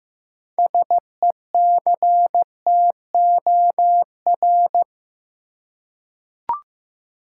Morse code, heard as a keyed tone, about 15 wpm, 700 Hz. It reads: SECTOR
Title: Six Letter Words - 15wpm